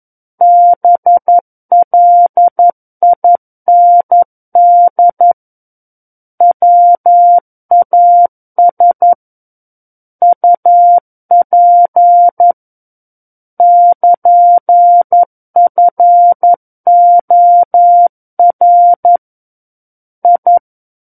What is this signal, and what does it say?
Morse code, heard as a keyed tone, about 11 wpm, 700 Hz. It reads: BLIND WAS UP <KN>FOR I
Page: War of the Worlds - 02-Chapter 2 - 11 WPM